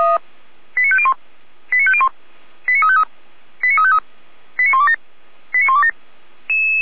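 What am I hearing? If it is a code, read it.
Five-tone selective call (ZVEI), heard as quick runs of five tones, tones of about 70 ms. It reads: 89501 89501 89353 89353 89127 89127
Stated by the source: ZVEI five-tone call received on an SDR, from feuerwehr-orlamuende.org.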